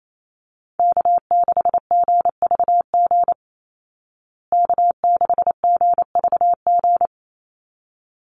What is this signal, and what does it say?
Morse code, heard as a keyed tone, about 28 wpm, 700 Hz. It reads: K6GVG K6GVG
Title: Call Signs Repeated 3x - 28wpm